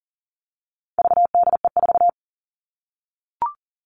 Morse code, heard as a keyed tone, about 40 wpm, 700 Hz. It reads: VDE4
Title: All Cut Numbers - 4-Digit 40wpm